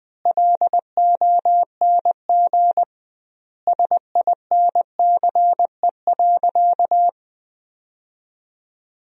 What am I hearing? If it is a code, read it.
Morse code, heard as a keyed tone, about 20 wpm, 700 Hz. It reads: LONG SINCE.
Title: 2x Word-Spacing - Sentences from Top 300 Words - 20wpm